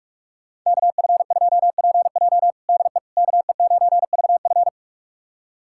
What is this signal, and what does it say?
Morse code, heard as a keyed tone, about 46 wpm, 700 Hz. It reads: KF2PJ DE KE9VF
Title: USA Callsign DE Callsign - 5 Characters - 45wpm